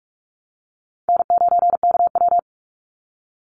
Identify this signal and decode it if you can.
Morse code, heard as a keyed tone, about 44 wpm, 700 Hz. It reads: N9KW
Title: NA SOTA Chaser Callsigns 45wpm